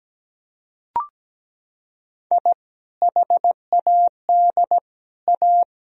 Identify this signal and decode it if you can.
Morse code, heard as a keyed tone, about 17 wpm, 700 Hz. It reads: I HAD A
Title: Sentences from Top 1000 Words - Review Part 1 - 17wpm